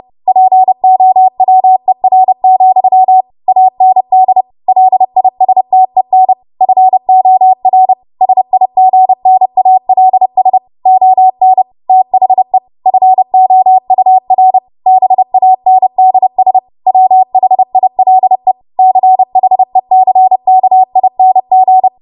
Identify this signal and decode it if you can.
Morse code, heard as a keyed tone, about 30 wpm, 750 Hz. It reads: POWER, AND LISTEN FOR SIGNALS ON THE FOUR BANDS WHILE CHECKING
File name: Archive / 30 WPM / 30WPM